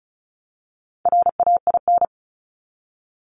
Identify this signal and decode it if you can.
Morse code, heard as a keyed tone, about 34 wpm, 700 Hz. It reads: RAIN
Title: Multi-letter Phonemes Repeated 3x 35wpm